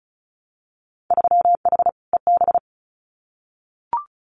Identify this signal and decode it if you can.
Morse code, heard as a keyed tone, about 36 wpm, 700 Hz. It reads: 3H EB
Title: Winter Field Day Exchanges Encore 35wpm